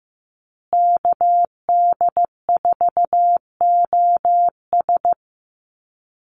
Morse code, heard as a keyed tone, about 15 wpm, 700 Hz. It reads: KD4OS
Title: Call Signs Repeated 3x - 15wpm